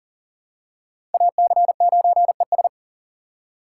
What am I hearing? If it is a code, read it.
Morse code, heard as a keyed tone, about 40 wpm, 700 Hz. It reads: AC9ES
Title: USA Call Signs - 5 Characters - 6x Character Spacing - 40wpm